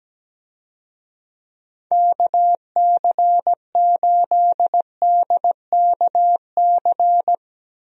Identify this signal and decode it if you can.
Morse code, heard as a keyed tone, about 17 wpm, 700 Hz. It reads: KC8DKC
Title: USA Call Signs - 6 Characters - Repeated 3x - 17wpm